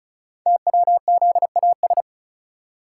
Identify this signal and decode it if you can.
Morse code, heard as a keyed tone, about 36 wpm, 690 Hz. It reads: TWZAS
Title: Code Groups - 5 Letters - 35wpm